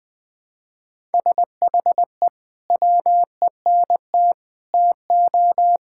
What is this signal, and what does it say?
Morse code, heard as a keyed tone, about 20 wpm, 700 Hz. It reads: SHE WENT TO
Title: Sentences with 2-Word Phrases - 20wpm